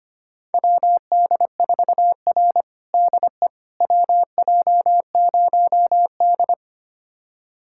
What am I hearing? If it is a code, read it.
Morse code, heard as a keyed tone, about 25 wpm, 700 Hz. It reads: WD4R DE WJ0D
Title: USA Callsign DE Callsign - 4 Characters - 25wpm